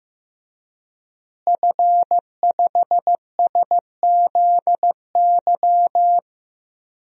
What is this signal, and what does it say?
Morse code, heard as a keyed tone, about 15 wpm, 700 Hz. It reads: F5SZY
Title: DXCC-100 Call Signs - 5 Characters - Repeated 3x - 15wpm